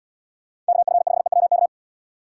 Morse code, heard as a keyed tone, about 78 wpm, 700 Hz. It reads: KF5EWK